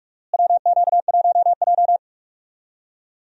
Morse code, heard as a keyed tone, about 46 wpm, 700 Hz. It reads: WQ1J
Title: USA Call Signs - 4 Characters - Repeated 3x - 45wpm